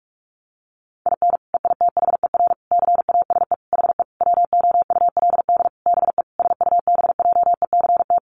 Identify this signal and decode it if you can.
Morse code, heard as a keyed tone, about 44 wpm, 700 Hz. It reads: IN EITHER CASE HE WOULD BE SUBJECT
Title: Sentences from Top 1300 Words 45wpm